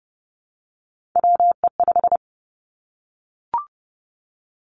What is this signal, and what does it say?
Morse code, heard as a keyed tone, about 30 wpm, 700 Hz. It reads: WE5
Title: Three Challenging Letter-Number Combinations 30wpm